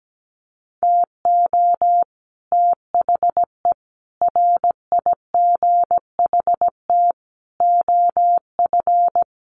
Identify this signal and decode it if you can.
Morse code, heard as a keyed tone, about 17 wpm, 700 Hz. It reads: TO THE RIGHT OF
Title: Sets of 5 Words Encore 17wpm